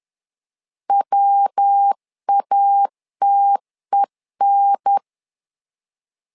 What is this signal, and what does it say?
Morse code, heard as a keyed tone, about 10 wpm, 800 Hz. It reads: WATEN
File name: LEZIONI / 3 MWRO / 3 MWR 10wpm 1952